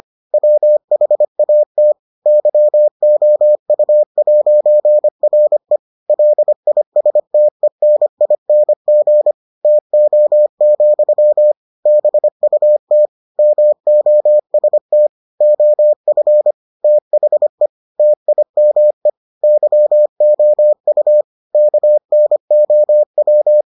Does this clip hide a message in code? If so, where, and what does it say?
Yes: Morse code from 0.3 to 23.6 s, heard as a keyed tone, about 25 wpm, 600 Hz. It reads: WHAT YOU'RE LISTENING TO, BUT MOST OF THE TIME YOU KNOW